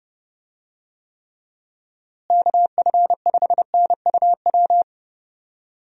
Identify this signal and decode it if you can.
Morse code, heard as a keyed tone, about 30 wpm, 700 Hz. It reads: KF5NUW
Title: USA Call Signs - 6 Characters - Repeated 2x - 30wpm